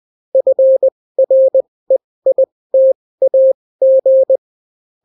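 Morse code, heard as a keyed tone, about 20 wpm, spaced as for 16 wpm, 530 Hz. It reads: FREITAG
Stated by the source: Daily Morse weather and top stories at 16 wpm, clean without QRN.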